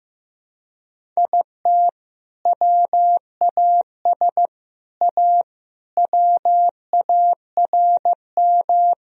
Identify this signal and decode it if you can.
Morse code, heard as a keyed tone, about 15 wpm, 700 Hz. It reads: IT WAS A WARM
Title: Sentences from Top 1000 Words - Review Part 2 - 15wpm